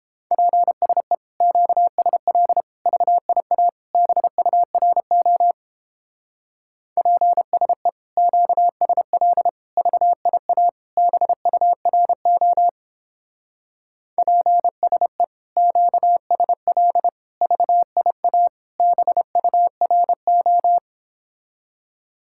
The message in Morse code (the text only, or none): PSE QSL VIA BURO PSE QSL VIA BURO PSE QSL VIA BURO